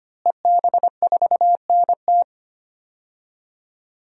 Morse code, heard as a keyed tone, about 25 wpm, 700 Hz. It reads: EB4NT
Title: DXCC-100 Call Signs - 5 Characters - Repeated 2x - 25wpm